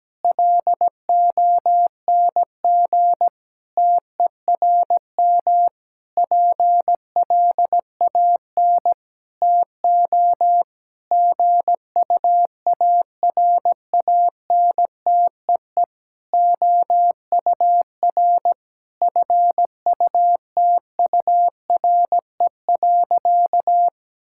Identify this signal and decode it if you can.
Morse code, heard as a keyed tone, about 17 wpm, 700 Hz. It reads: LONG TERM PLAN TO GUARANTEE OUR FUTURE.
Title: Sentences from Top 1600 Words 17wpm